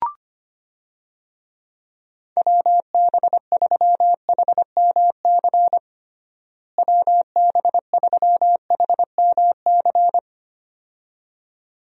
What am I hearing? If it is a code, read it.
Morse code, heard as a keyed tone, about 25 wpm, 700 Hz. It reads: WB3HMC WB3HMC
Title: USA Call Signs - 6 Characters - Repeated 2x - 25wpm